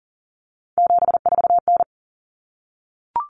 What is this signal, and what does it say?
Morse code, heard as a keyed tone, about 40 wpm, 700 Hz. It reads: 74N
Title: Common Cut Numbers - 3-Digit 40wpm